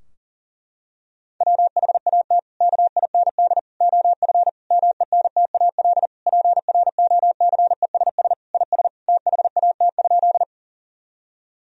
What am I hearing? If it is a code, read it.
Morse code, heard as a keyed tone, about 40 wpm, 700 Hz. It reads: WHAT KIND OF MENTAL PROCESS IS THAT?
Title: Sentences from Top 1000 Words - Review Part 4 - 40wpm